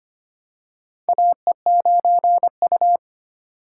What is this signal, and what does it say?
Morse code, heard as a keyed tone, about 25 wpm, 700 Hz. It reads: AE9U
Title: USA Call Signs - 4 Characters - 4x Character Spacing - 25wpm